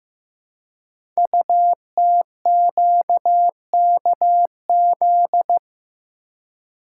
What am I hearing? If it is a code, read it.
Morse code, heard as a keyed tone, about 15 wpm, 700 Hz. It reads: UTQKZ
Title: Code Groups - 5 Letters - 15wpm